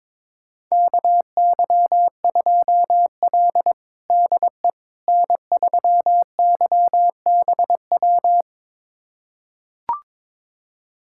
Morse code, heard as a keyed tone, about 22 wpm, 700 Hz. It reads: KY2L DE N3YBW
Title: USA Callsign DE Callsign 22wpm